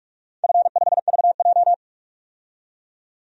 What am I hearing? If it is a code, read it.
Morse code, heard as a keyed tone, about 44 wpm, 690 Hz. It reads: F5VJ
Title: DXCC-100 Call Signs - 4 Characters - 45wpm